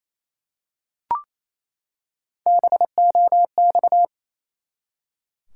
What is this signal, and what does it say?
Morse code, heard as a keyed tone, about 28 wpm, 700 Hz. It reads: BOX
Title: Three Letter Words - 28wpm